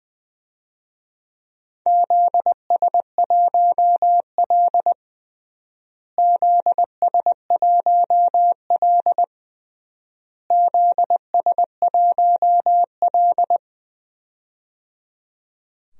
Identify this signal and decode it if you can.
Morse code, heard as a keyed tone, about 21 wpm, 700 Hz. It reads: ZS1L ZS1L ZS1L
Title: DXCC-100 Call Signs - 4 Characters - Repeated 3x - 20wpm